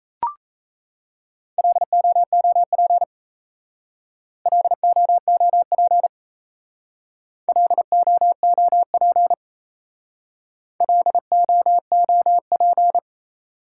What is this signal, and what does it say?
Morse code, heard as a keyed tone, about 36 wpm, 700 Hz. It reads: LOOP LOOP LOOP LOOP